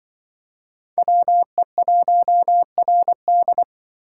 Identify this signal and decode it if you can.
Morse code, heard as a keyed tone, about 25 wpm, 700 Hz. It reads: WE1RD
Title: Speed Racing - USA 2x2 Call Signs 20wpm